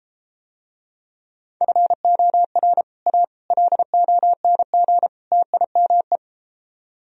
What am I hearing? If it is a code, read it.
Morse code, heard as a keyed tone, about 33 wpm, 700 Hz. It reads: FOR A LONG TIME